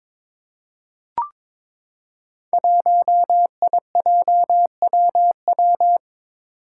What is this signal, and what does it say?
Morse code, heard as a keyed tone, about 22 wpm, 700 Hz. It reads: 1IJWW
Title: Five Challenging Letter-Number Combinations 22wpm